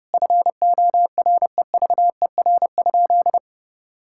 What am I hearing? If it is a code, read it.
Morse code, heard as a keyed tone, about 30 wpm, 700 Hz. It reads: FOREVER?
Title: Sentences from Top 1600 Words 30wpm